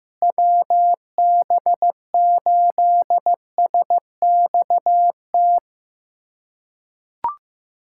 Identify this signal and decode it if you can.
Morse code, heard as a keyed tone, about 15 wpm, 700 Hz. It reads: WB8SXT
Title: USA Callsign DE Callsign - 6 Characters - 15wpm